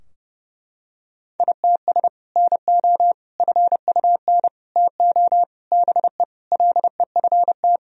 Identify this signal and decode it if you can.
Morse code, heard as a keyed tone, about 30 wpm, 700 Hz. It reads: ITS NO FUN TO BE LEFT
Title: Sentences from Top 1000 Words - Review Part 4 - 30wpm